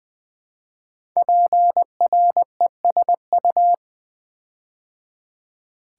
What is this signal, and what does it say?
Morse code, heard as a keyed tone, about 20 wpm, 700 Hz. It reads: PRESU
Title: Build A Word - 5 Letters At A Time - 10-Letter Words - 20wpm